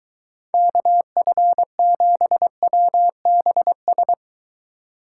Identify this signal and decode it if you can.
Morse code, heard as a keyed tone, about 23 wpm, 700 Hz. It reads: KF7WBS